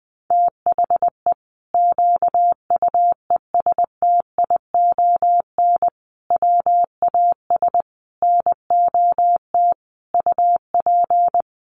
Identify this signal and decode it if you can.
Morse code, heard as a keyed tone, about 20 wpm, 700 Hz. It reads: THE QUESTION WAS NOT UP
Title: Sentences from Top 1200 Words 20wpm